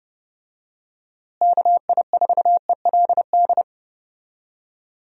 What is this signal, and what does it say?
Morse code, heard as a keyed tone, about 30 wpm, 700 Hz. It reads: KI4ELD
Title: USA Call Signs - 6 Characters - Repeated 3x - 30wpm